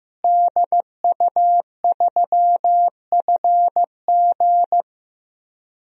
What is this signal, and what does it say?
Morse code, heard as a keyed tone, about 15 wpm, 700 Hz. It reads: DU3FG
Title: DXCC-100 Call Signs - 5 Characters - Repeated 2x - 15wpm